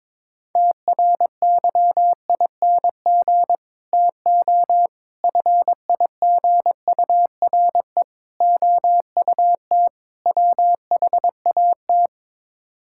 Morse code, heard as a keyed tone, about 22 wpm, 700 Hz. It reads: TRYING TO FIGURE OUT WHAT
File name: practice / 22wpm / Build A Phrase - 5-Word Phrases - 22wpm